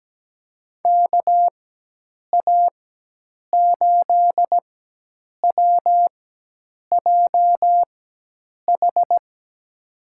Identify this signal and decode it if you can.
Morse code, heard as a keyed tone, about 17 wpm, spaced as for 8 wpm, 700 Hz. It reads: KA8WJH
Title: USA Call Signs - 6 Characters - 4x Character Spacing - 17wpm